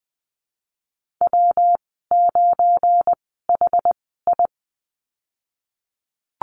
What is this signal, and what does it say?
Morse code, heard as a keyed tone, about 20 wpm, spaced as for 15 wpm, 700 Hz. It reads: W9HI
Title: 179 Lesson 29g - Slash - Call Signs 20wpm